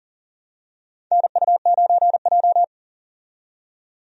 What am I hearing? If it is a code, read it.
Morse code, heard as a keyed tone, about 39 wpm, 700 Hz. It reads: NU9J
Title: USA Call Signs - 4 Characters - 40wpm